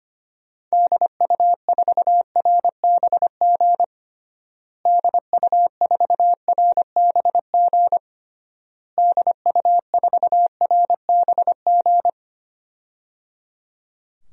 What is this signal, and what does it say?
Morse code, heard as a keyed tone, about 25 wpm, 700 Hz. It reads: DU4RBG DU4RBG DU4RBG
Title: DXCC-100 Call Signs - 6 Characters - Repeated 3x - 25wpm